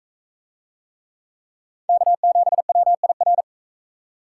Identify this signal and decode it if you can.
Morse code, heard as a keyed tone, about 37 wpm, 700 Hz. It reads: K7WIR